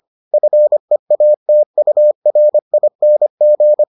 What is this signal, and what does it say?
Morse code, heard as a keyed tone, about 25 wpm, 600 Hz. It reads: FEATURING